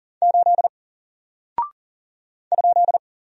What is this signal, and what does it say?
Morse code, heard as a keyed tone, about 40 wpm, 700 Hz. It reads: 8 ?
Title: Single Character (Letter:Number:Punctuation:Prosign) 40wpm